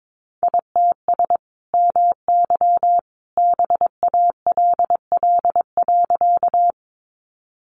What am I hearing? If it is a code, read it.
Morse code, heard as a keyed tone, about 22 wpm, 700 Hz. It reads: ITS MY BALL.
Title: Sentences from Top 900 Words 22wpm